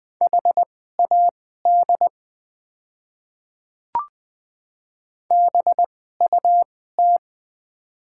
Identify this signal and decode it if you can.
Morse code, heard as a keyed tone, about 20 wpm, spaced as for 15 wpm, 700 Hz. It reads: HAD BUT
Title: 2x Character Spacing - Top 100 Words - 20wpm